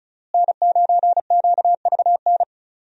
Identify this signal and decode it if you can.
Morse code, heard as a keyed tone, about 34 wpm, 700 Hz. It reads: N9QVN
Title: USA Call Signs - 5 Characters - Repeated 2x - 35wpm